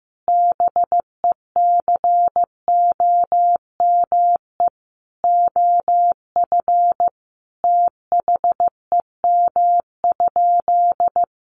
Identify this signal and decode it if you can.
Morse code, heard as a keyed tone, about 15 wpm, 700 Hz. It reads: BECOME OF THEM?
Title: Sentences from Top 600 Words 15wpm